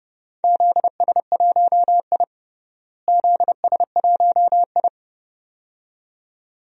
Morse code, heard as a keyed tone, about 32 wpm, 700 Hz. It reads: ZS1I ZS1I
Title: DXCC-100 Call Signs - 4 Characters - Repeated 2x - 30wpm